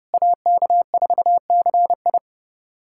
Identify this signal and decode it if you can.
Morse code, heard as a keyed tone, about 30 wpm, 700 Hz. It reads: AK4CI
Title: USA Callsign DE Callsign - 5 Characters - 30wpm